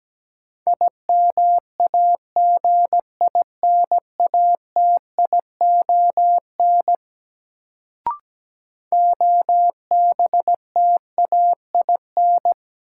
Morse code, heard as a keyed tone, about 17 wpm, 700 Hz. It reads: IMAGINATION OBTAIN
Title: Sentences from Top 1300 Words - New Word Drill Alt - 17wpm